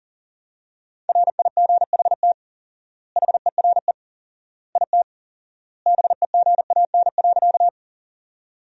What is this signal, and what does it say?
Morse code, heard as a keyed tone, about 40 wpm, 700 Hz. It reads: RIGHT HERE IT BEGAN.
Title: 4x Word-Spacing - Sentences from Top 300 Words - 40wpm